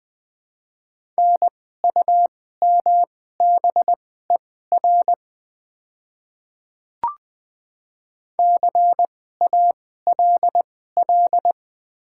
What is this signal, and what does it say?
Morse code, heard as a keyed tone, about 20 wpm, spaced as for 15 wpm, 700 Hz. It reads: NUMBER CALL
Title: 2x Character Spacing - Top 100 Words - 20wpm